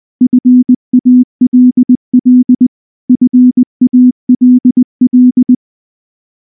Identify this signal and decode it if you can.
Morse code, heard as a keyed tone, about 19 wpm, 250 Hz. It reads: FALL FALL
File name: morse / morse-training-20WPM-6